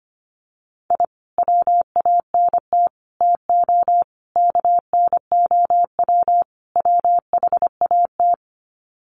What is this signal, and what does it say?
Morse code, heard as a keyed tone, about 25 wpm, 700 Hz. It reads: I WANT TO KNOW WHAT
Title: Sets of 5 Words Encore2 25wpm